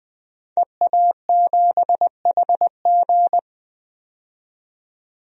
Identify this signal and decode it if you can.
Morse code, heard as a keyed tone, about 20 wpm, 700 Hz. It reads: EA7HG
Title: DXCC-100 Call Signs - 5 Characters - 20wpm